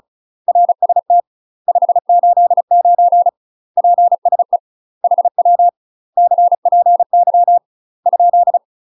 Morse code, heard as a keyed tone, about 34 wpm, 700 Hz. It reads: RST 589 PSE HW CPY ?
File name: UK MASTER QSOs TRAINING/35WPM